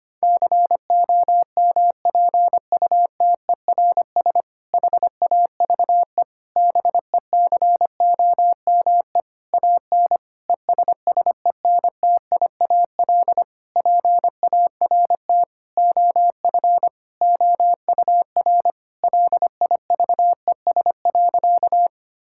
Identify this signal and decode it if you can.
Morse code, heard as a keyed tone, about 25 wpm, 700 Hz. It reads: COMPUTERS HAVE BECOME AN ESSENTIAL PART OF OUR LIVES.
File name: practice / 25wpm / Difficult Sentences with Hints - 25wpm